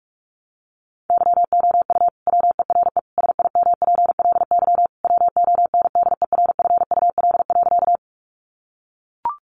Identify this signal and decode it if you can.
Morse code, heard as a keyed tone, about 44 wpm, 700 Hz. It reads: YOU WERE SIMPLY WONDERFUL.
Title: Sentences from Top 1100 Words 45wpm